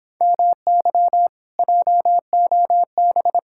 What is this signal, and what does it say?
Morse code, heard as a keyed tone, about 26 wpm, 700 Hz. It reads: MY JOB